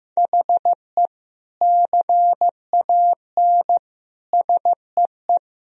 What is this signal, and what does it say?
Morse code, heard as a keyed tone, about 16 wpm, 700 Hz. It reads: HE CAN SEE
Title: Sentences from Top 1000 Words - Review Part 2 - 15wpm